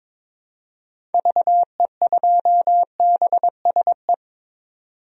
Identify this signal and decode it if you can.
Morse code, heard as a keyed tone, about 22 wpm, 700 Hz. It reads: VE2BSE
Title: DXCC-100 Call Signs - 6 Characters - 22wpm